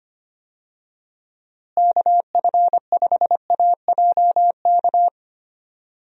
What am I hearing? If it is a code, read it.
Morse code, heard as a keyed tone, about 25 wpm, 700 Hz. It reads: KF5AJK
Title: USA Call Signs - 6 Characters - Repeated 2x - 25wpm